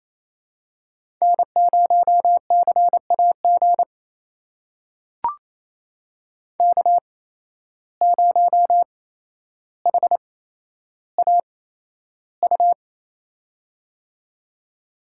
Morse code, heard as a keyed tone, about 28 wpm, spaced as for 12 wpm, 700 Hz. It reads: N0CAG K0HAU
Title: USA Call Signs - 5 Characters - 8x Character Spacing - 28wpm